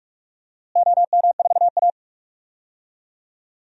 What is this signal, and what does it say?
Morse code, heard as a keyed tone, about 46 wpm, 700 Hz. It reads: OM4A
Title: DXCC-100 Call Signs - 4 Characters - Repeated 2x - 45wpm